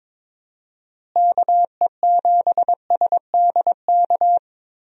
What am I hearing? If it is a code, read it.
Morse code, heard as a keyed tone, about 22 wpm, 700 Hz. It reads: KE7SDK
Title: USA Call Signs - 6 Characters - Repeated 2x - 22wpm